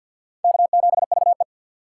Morse code, heard as a keyed tone, about 52 wpm, 690 Hz. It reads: K7VE
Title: USA Call Signs - 2x Character Spacing - 50wpm